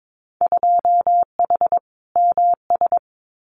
Morse code, heard as a keyed tone, about 22 wpm, 700 Hz. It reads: 2H MS
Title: Winter Field Day Exchanges 22wpm